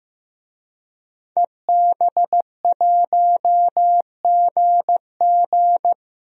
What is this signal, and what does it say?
Morse code, heard as a keyed tone, about 15 wpm, 700 Hz. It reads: EB1GG
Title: DXCC-100 Call Signs - 5 Characters - Repeated 2x - 15wpm